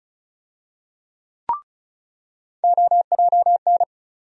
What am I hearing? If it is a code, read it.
Morse code, heard as a keyed tone, about 34 wpm, 700 Hz. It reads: OJN
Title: USA Call Sign Postfixes - 3 Letters 35wpm